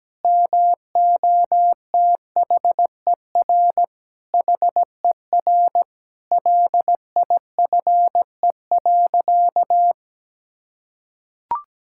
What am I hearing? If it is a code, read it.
Morse code, heard as a keyed tone, about 17 wpm, 700 Hz. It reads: MOTHER HER LIFE.
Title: Sentences from Top 1000 Words - Review Part 5 - 17wpm